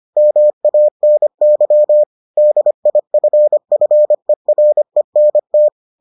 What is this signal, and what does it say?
Morse code, heard as a keyed tone, about 25 wpm, 600 Hz. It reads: MANY DIFFERENT